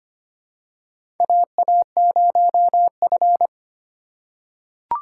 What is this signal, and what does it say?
Morse code, heard as a keyed tone, about 25 wpm, 700 Hz. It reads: AA0F
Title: USA Call Signs - 3x Character Spacing - 25wpm